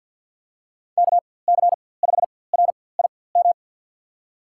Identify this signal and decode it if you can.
Morse code, heard as a keyed tone, about 52 wpm, spaced as for 24 wpm, 700 Hz. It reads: KC5RIM
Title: USA Call Signs - 4x Character Spacing - 50wpm